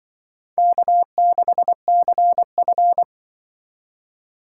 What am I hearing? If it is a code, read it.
Morse code, heard as a keyed tone, about 25 wpm, 700 Hz. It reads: K6CF